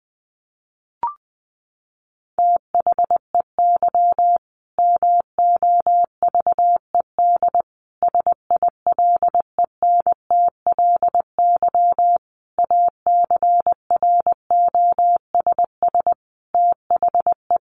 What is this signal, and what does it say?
Morse code, heard as a keyed tone, about 20 wpm, 700 Hz. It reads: THEY MOVED SILENTLY ACROSS THE
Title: A Sound Of Thunder 20wpm